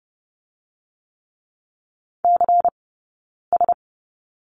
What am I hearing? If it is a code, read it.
Morse code, heard as a keyed tone, about 30 wpm, 700 Hz. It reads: C S
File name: Lesson 16d - Letter c - Two Characters 30wpm